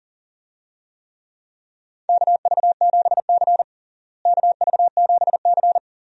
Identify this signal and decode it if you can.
Morse code, heard as a keyed tone, about 40 wpm, 700 Hz. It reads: KV7C KV7C
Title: USA Call Signs - 4 Characters - Repeated 3x - 40wpm